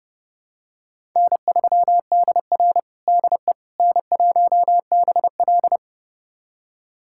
Morse code, heard as a keyed tone, about 32 wpm, 700 Hz. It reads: N3DR DE N1BL
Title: USA Callsign DE Callsign - 4 Characters - 30wpm